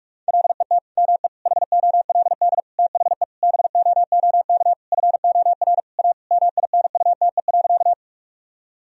Morse code, heard as a keyed tone, about 44 wpm, 700 Hz. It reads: LET ME HOLD THE BOOK FOR A MINUTE.
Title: Sentences from Top 1000 Words - Review Part 2 - 45wpm